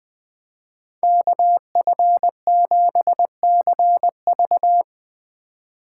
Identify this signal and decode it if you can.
Morse code, heard as a keyed tone, about 20 wpm, 700 Hz. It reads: KF7CV
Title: USA Call Signs - 5 Characters - 3x Character Spacing - 20wpm